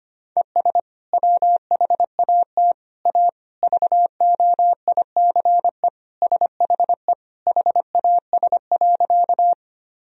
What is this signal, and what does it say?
Morse code, heard as a keyed tone, about 25 wpm, 700 Hz. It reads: ES WHAT A VOICE SHE HAS.
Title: Sentences from Top 500 Words and Abbreviations 25wpm